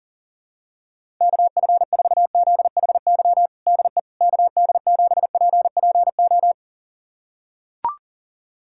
Morse code, heard as a keyed tone, about 40 wpm, 700 Hz. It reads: KF4ZHY DE KD7PPO
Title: USA Callsign DE Callsign 40wpm